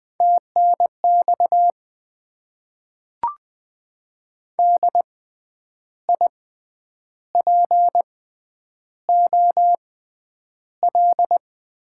Morse code, heard as a keyed tone, about 20 wpm, spaced as for 9 wpm, 700 Hz. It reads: TNX DIPOL
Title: Top 90 QSO Elements - 6x Character Spacing - 20wpm